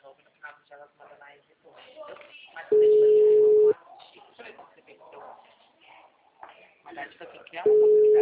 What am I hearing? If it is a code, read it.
Ringback tone (Europe, 425 Hz), with speech.